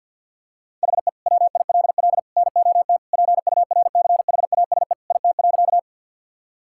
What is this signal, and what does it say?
Morse code, heard as a keyed tone, about 52 wpm, 700 Hz. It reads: HE WILL NOT PURCHASE IT.
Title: Sentences from Top 1000 Words - Review Part 5 - 50wpm